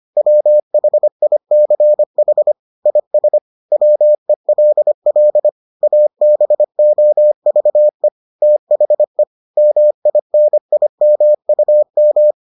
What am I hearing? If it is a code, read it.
Morse code, heard as a keyed tone, about 25 wpm, 600 Hz. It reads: WHICH IS WELL ABOVE THE MINIMUM